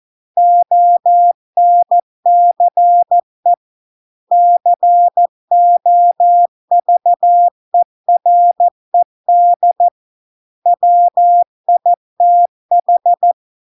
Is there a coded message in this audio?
Yes: Morse code from 0.4 to 13.3 s, heard as a keyed tone, about 14 wpm, 700 Hz. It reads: ONCE COVERED WITH